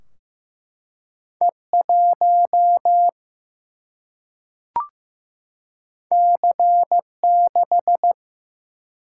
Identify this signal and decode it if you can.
Morse code, heard as a keyed tone, about 15 wpm, 700 Hz. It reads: E 1 C 6
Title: Single-Letter Single-Number 15wpm